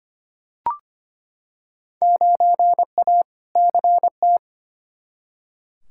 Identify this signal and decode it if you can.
Morse code, heard as a keyed tone, about 25 wpm, 700 Hz. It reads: 9A CT
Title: Field Day Exchanges Encore - 25wpm